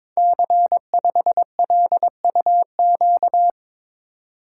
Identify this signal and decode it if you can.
Morse code, heard as a keyed tone, about 22 wpm, 700 Hz. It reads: C5LUQ
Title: Code Groups - 5 Letter:Numbers - 22wpm